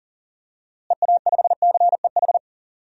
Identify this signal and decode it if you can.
Morse code, heard as a keyed tone, about 40 wpm, 700 Hz. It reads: EA5CEH
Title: DXCC-100 Call Signs - 6 Characters - Repeated 2x - 40wpm